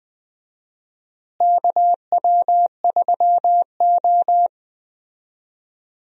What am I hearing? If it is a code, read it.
Morse code, heard as a keyed tone, about 20 wpm, 700 Hz. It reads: KW3O
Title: USA Call Signs - 4 Characters - Repeated 3x - 20wpm